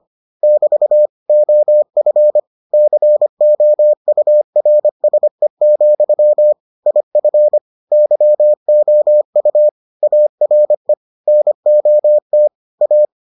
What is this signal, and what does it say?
Morse code, heard as a keyed tone, about 25 wpm, 600 Hz. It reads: <BT> OF COURSE, IF YOU ARE NOT A